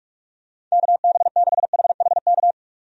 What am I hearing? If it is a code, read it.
Morse code, heard as a keyed tone, about 41 wpm, 700 Hz. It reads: KB6HHK
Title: USA Call Signs - 6 Characters - 45wpm